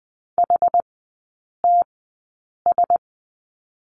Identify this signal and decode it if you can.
Morse code, heard as a keyed tone, about 20 wpm, spaced as for 8 wpm, 700 Hz. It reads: HTS
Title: 107 Lesson 18e - Letter m - Three Characters 20wpm